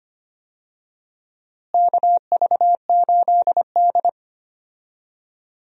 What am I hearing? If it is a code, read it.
Morse code, heard as a keyed tone, about 25 wpm, 700 Hz. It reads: KV8D
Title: USA Call Signs - 4 Characters - Repeated 2x - 25wpm